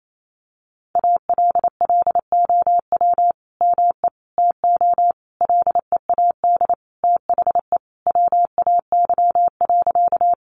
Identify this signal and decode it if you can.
Morse code, heard as a keyed tone, about 28 wpm, 700 Hz. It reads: ALLOW ME TO LEAD THE WAY.
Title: Sentences from Top 700 Words 28wpm